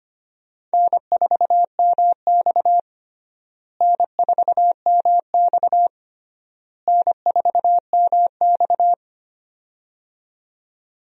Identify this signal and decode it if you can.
Morse code, heard as a keyed tone, about 25 wpm, 700 Hz. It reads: N4MX N4MX N4MX
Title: USA Call Signs - 4 Characters - Repeated 3x - 25wpm